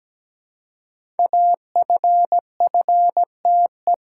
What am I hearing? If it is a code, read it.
Morse code, heard as a keyed tone, about 17 wpm, 700 Hz. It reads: AFFTE
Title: Code Groups - 5 Letters - 17wpm